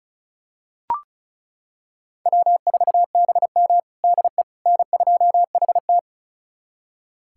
Morse code, heard as a keyed tone, about 36 wpm, 700 Hz. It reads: W4BM DE N2HT
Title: USA Callsign DE Callsign - 4 Characters - 35wpm